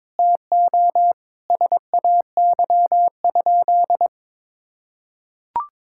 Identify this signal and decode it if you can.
Morse code, heard as a keyed tone, about 22 wpm, 700 Hz. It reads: TO SAY?
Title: Sentences from Top 1000 Words - Review Part 6 - 22wpm